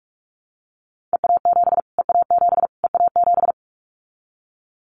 Morse code, heard as a keyed tone, about 44 wpm, 700 Hz. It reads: EA7 EA7 EA7
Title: Callsign Prefixes for Top 25 DXCC - Repeated 3x - 45wpm